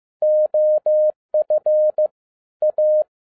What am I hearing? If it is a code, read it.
Morse code, heard as a keyed tone, about 15 wpm, 610 Hz. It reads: OF A